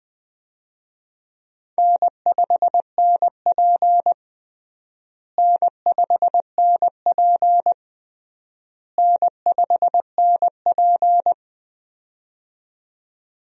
Morse code, heard as a keyed tone, about 20 wpm, 700 Hz. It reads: N5NP N5NP N5NP
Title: USA Call Signs - 4 Characters - Repeated 3x - 20wpm